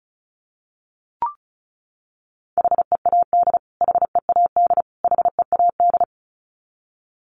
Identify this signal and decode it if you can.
Morse code, heard as a keyed tone, about 34 wpm, 700 Hz. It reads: HEAD HEAD HEAD
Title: Top 500 Words Repeated 35wpm